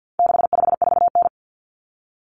Morse code, heard as a keyed tone, about 50 wpm, 700 Hz. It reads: 654N
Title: Common Cut Numbers - 4-Digit 50wpm